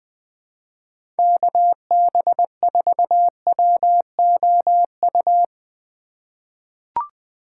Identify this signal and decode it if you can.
Morse code, heard as a keyed tone, about 21 wpm, 700 Hz. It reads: KB4WOU
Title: USA Call Signs - 6 Characters - 3x Character Spacing - 20wpm